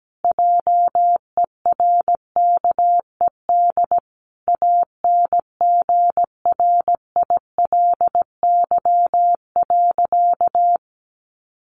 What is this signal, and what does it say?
Morse code, heard as a keyed tone, about 17 wpm, 700 Hz. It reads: JERKED ANGRILY.
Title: A Sound Of Thunder 17wpm